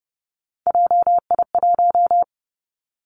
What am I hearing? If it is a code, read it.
Morse code, heard as a keyed tone, about 30 wpm, 700 Hz. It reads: JI1
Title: Three Challenging Letter-Number Combinations 30wpm